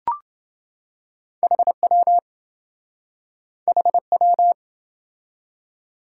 Morse code, heard as a keyed tone, about 27 wpm, 700 Hz. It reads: HW HW